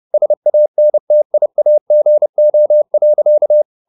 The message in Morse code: SANTIAGO.

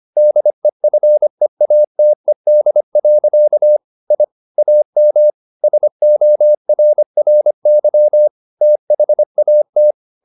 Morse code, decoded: DEFEATED. I AM SORRY THAT

